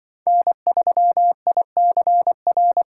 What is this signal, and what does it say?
Morse code, heard as a keyed tone, about 24 wpm, 700 Hz. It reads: N3ICR
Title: Speed Racing - USA 1x3 Call Signs 20wpm